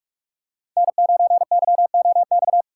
Morse code, heard as a keyed tone, about 46 wpm, 700 Hz. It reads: N9YOX
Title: USA Call Signs - 5 Characters - 6x Character Spacing - 45wpm